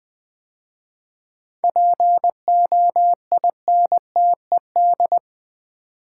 Morse code, heard as a keyed tone, about 20 wpm, 700 Hz. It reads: POINTED
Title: 7 Letter Words - First 3 Letters Given - 20wpm